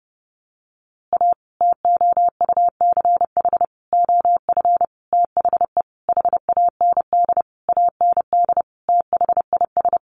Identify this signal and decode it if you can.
Morse code, heard as a keyed tone, about 30 wpm, 700 Hz. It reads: A TOUCH OF THE HAND AND THIS
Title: A Sound Of Thunder 30wpm